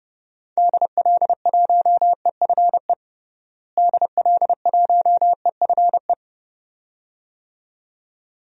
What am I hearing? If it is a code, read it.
Morse code, heard as a keyed tone, about 32 wpm, 700 Hz. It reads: DL1EFE DL1EFE
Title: DXCC-100 Call Signs - 6 Characters - Repeated 2x - 30wpm